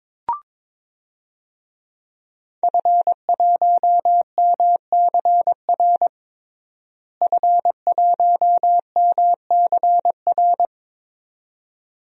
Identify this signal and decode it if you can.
Morse code, heard as a keyed tone, about 22 wpm, 700 Hz. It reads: F1MCR F1MCR
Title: DXCC-100 Call Signs - 5 Characters - Repeated 2x - 22wpm